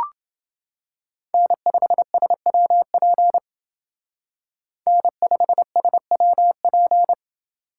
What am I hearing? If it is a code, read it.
Morse code, heard as a keyed tone, about 28 wpm, 700 Hz. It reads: N5SWP N5SWP